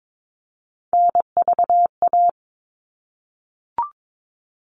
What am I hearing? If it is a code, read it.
Morse code, heard as a keyed tone, about 22 wpm, 700 Hz. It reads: NVA
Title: All Cut Numbers - 3-Digit 22wpm